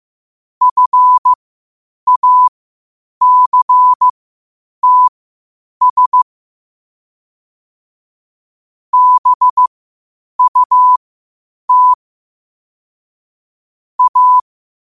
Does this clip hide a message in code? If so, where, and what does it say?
Yes: Morse code from 0.6 to 14.4 s, heard as a keyed tone, about 15 wpm, spaced as for 8 wpm, 1000 Hz. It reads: FACTS BUT A